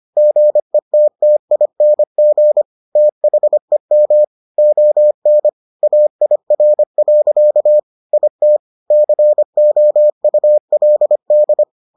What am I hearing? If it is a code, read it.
Morse code, heard as a keyed tone, about 25 wpm, 600 Hz. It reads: GETTING THEM ON AIR. IT COULD